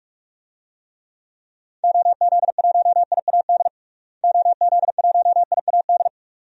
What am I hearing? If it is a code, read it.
Morse code, heard as a keyed tone, about 42 wpm, 700 Hz. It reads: OZ1IAD OZ1IAD
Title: DXCC-100 Call Signs - 6 Characters - Repeated 2x - 45wpm